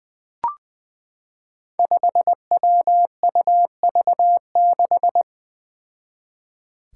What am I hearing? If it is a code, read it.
Morse code, heard as a keyed tone, about 20 wpm, 700 Hz. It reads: 5WUV6
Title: Five Challenging Letter-Number Combinations 20wpm